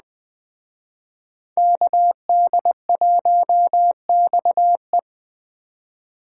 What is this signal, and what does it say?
Morse code, heard as a keyed tone, about 21 wpm, 700 Hz. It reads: KD1XE